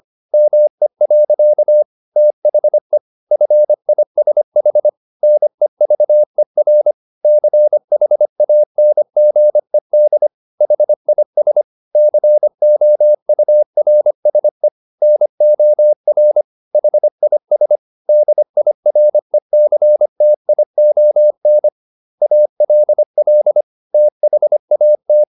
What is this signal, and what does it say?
Morse code, heard as a keyed tone, about 25 wpm, 600 Hz. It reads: ME. THE FISH NEVER CHANGED HIS COURSE NOR HIS DIRECTION ALL THAT
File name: the old man and the sea part 1 25wpm